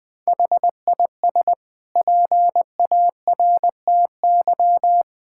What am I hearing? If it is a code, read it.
Morse code, heard as a keyed tone, about 20 wpm, 700 Hz. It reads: HIS PARTY
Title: Sentences from Top 1600 Words 20wpm